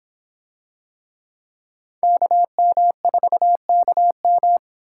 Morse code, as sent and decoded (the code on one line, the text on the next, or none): -.- -- ....- -.- --
KM4KM